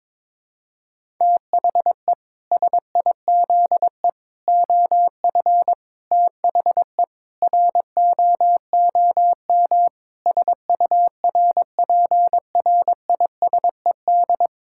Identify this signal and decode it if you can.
Morse code, heard as a keyed tone, about 22 wpm, 700 Hz. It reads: THE SIZE OF THE ROOM SURPRISED
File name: practice / 22wpm / Sentences with 4-Word Phrases - 22wpm